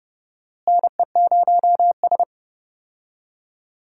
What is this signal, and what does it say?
Morse code, heard as a keyed tone, about 30 wpm, 700 Hz. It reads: NE0S
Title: USA Call Signs - 4 Characters - Repeated 3x - 30wpm